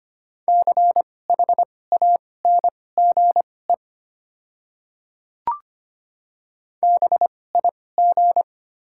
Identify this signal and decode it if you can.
Morse code, heard as a keyed tone, about 25 wpm, spaced as for 18 wpm, 700 Hz. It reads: CHANGE BIG
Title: 2x Character Spacing - Top 500 Words - 25wpm